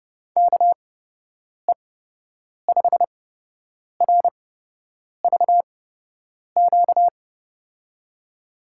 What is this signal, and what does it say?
Morse code, heard as a keyed tone, about 30 wpm, spaced as for 8 wpm, 700 Hz. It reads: KE5RVQ
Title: USA Call Signs - 6 Characters - 8x Character Spacing - 30wpm